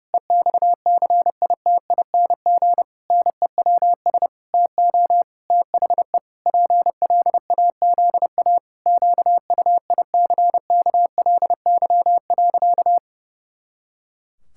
Morse code, heard as a keyed tone, about 30 wpm, 700 Hz. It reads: EXCITING NEWS TO THE PLAZA QUICKLY.
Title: Pangram Challenge v2 - 30wpm